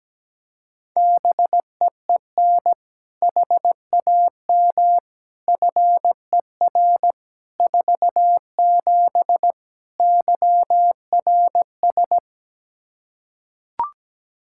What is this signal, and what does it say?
Morse code, heard as a keyed tone, about 17 wpm, 700 Hz. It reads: BEEN HAM FER 47 YRS
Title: Practice QSO - Exchange 3 - More Info - 17wpm